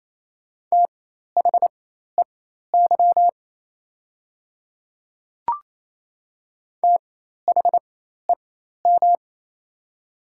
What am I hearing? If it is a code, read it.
Morse code, heard as a keyed tone, about 27 wpm, spaced as for 13 wpm, 700 Hz. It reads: THEY THEM
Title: 4x Character Spacing - Top 100 Words - 28wpm